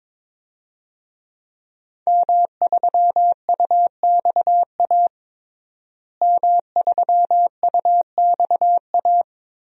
Morse code, heard as a keyed tone, about 22 wpm, 700 Hz. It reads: M3UXA M3UXA
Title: DXCC-100 Call Signs - 5 Characters - Repeated 2x - 22wpm